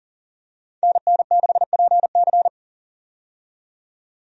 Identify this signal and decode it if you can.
Morse code, heard as a keyed tone, about 40 wpm, 700 Hz. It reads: NN6PC
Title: USA Call Signs - 5 Characters - 40wpm